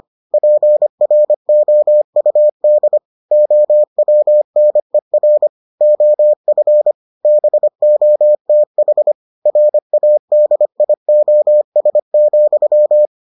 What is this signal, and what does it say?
Morse code, heard as a keyed tone, about 25 wpm, 600 Hz. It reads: PROUD OWNER OF BOTH RADIOS,